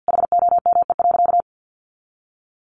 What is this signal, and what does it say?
Morse code, heard as a keyed tone, about 50 wpm, 700 Hz. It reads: HOME.
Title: Sentences from Top 1400 Words 50wpm